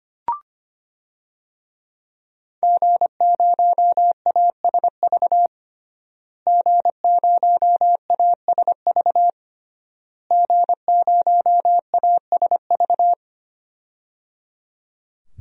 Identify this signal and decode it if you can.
Morse code, heard as a keyed tone, about 25 wpm, 700 Hz. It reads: G0ASV G0ASV G0ASV
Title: DXCC-100 Call Signs - 5 Characters - Repeated 3x - 25wpm